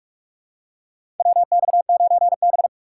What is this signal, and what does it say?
Morse code, heard as a keyed tone, about 45 wpm, 700 Hz. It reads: WX9B
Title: USA Call Signs - 4 Characters - 4x Character Spacing - 45wpm